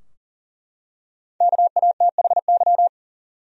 Morse code, heard as a keyed tone, about 40 wpm, 700 Hz. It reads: KATHY
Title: USA Common Names - Max 5 Letters - 40wpm